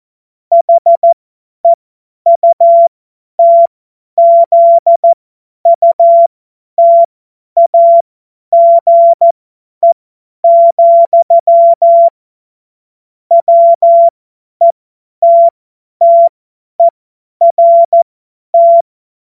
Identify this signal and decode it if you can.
Morse code, heard as a keyed tone, about 14 wpm, spaced as for 10 wpm, 680 Hz. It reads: HEUTZUTAGE, WETTERT
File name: Morsekurs / Witze / Witz 15. November 2022 50 F70